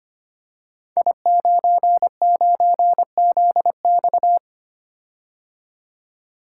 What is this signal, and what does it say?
Morse code, heard as a keyed tone, about 25 wpm, 700 Hz. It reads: I99ZX
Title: Code Groups - 5 Letter:Numbers - 25wpm